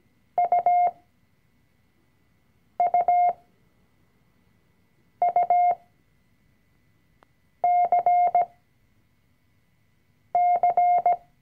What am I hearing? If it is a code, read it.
Morse code, heard as a keyed tone, about 17 wpm, spaced as for 4 wpm, 710 Hz. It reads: UUUCC
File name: CW-Online-Kurs / Anfänger / Lektion 004